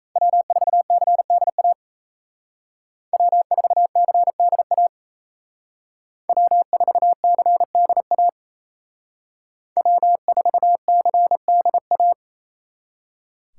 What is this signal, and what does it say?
Morse code, heard as a keyed tone, about 33 wpm, 700 Hz. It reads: W4CDA W4CDA W4CDA W4CDA
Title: Speed Racing - USA 1x3 Call Signs 28wpm